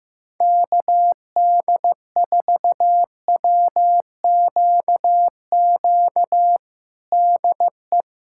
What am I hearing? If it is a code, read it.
Morse code, heard as a keyed tone, about 15 wpm, 700 Hz. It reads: KD4WQQ DE
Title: USA Callsign DE Callsign - 6 Characters - 15wpm